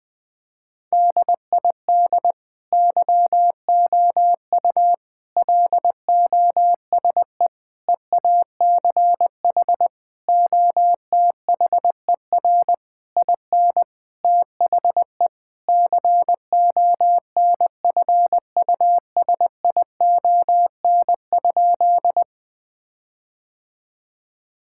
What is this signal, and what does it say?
Morse code, heard as a keyed tone, about 20 wpm, 700 Hz. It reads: DID YOU LOSE EACH OTHER IN THE CONFUSION?
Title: Sentences from Top 1600 Words 20wpm